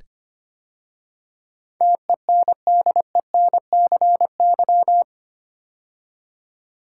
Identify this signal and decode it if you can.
Morse code, heard as a keyed tone, about 25 wpm, 690 Hz. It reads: TENDENCY
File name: 8 Letter Words - First 3 Letters Given - 25wpm